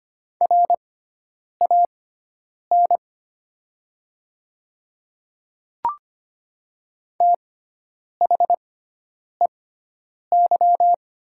Morse code, heard as a keyed tone, about 25 wpm, spaced as for 9 wpm, 700 Hz. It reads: RAN THEY
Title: 6x Character Spacing - Top 500 Words - 25wpm